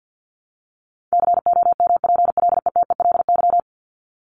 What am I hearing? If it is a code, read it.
Morse code, heard as a keyed tone, about 50 wpm, 700 Hz. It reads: COMPLETELY
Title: LY Suffix - Top 100 Words with Base Words 50wpm